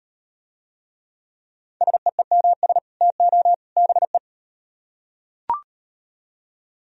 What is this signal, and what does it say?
Morse code, heard as a keyed tone, about 38 wpm, 700 Hz. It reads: SEEMS TO BE